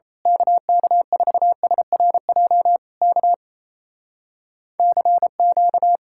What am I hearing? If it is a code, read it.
Morse code, heard as a keyed tone, about 26 wpm, 700 Hz. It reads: KK4SRJ K CQ
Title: Speed-Racing Common QSO Phrases - 28wpm